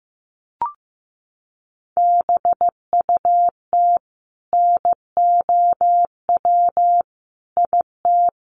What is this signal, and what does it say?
Morse code, heard as a keyed tone, about 15 wpm, 700 Hz. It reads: BUT NOW IT
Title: Sentences from Top 600 Words 15wpm